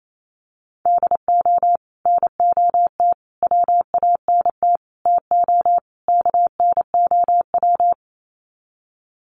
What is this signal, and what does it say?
Morse code, heard as a keyed tone, about 28 wpm, 700 Hz. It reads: DO NOT WANT TO KNOW
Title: Sets of 5 Words Encore 28wpm